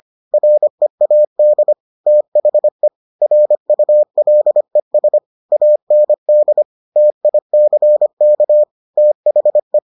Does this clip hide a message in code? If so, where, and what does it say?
Yes: Morse code from 0.3 to 9.8 s, heard as a keyed tone, about 25 wpm, 600 Hz. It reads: READ THE RULES AND TICK THE